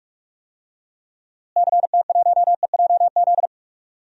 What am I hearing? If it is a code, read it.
Morse code, heard as a keyed tone, about 46 wpm, 700 Hz. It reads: CT1EJZ
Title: DXCC-100 Call Signs - 6 Characters - Repeated 2x - 45wpm